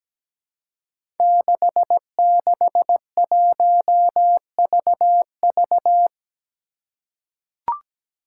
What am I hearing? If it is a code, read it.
Morse code, heard as a keyed tone, about 17 wpm, 700 Hz. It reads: 661VV
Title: Five Challenging Letter-Number Combinations 17wpm